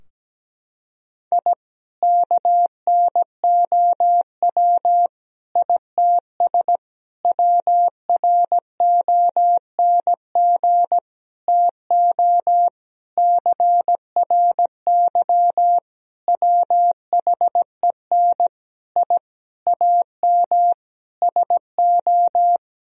Morse code, heard as a keyed tone, about 17 wpm, 700 Hz. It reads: I KNOW ITS WRONG TO CRY WHEN I AM SO
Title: Sentences from Top 1000 Words - Review Part 2 - 17wpm